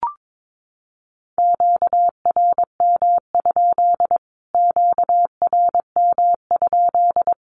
Morse code, heard as a keyed tone, about 22 wpm, 700 Hz. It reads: QRM? QRM?
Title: Q-Codes Repeated 3x 22wpm